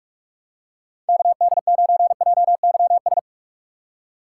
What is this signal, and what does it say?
Morse code, heard as a keyed tone, about 44 wpm, 700 Hz. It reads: KD9JYS
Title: USA Call Signs - 6 Characters - 6x Character Spacing - 45wpm